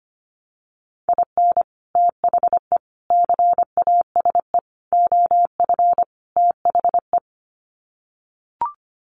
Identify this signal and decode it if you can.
Morse code, heard as a keyed tone, about 25 wpm, 700 Hz. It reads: IN THE CASE OF THE
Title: Sets of 5 Words 25wpm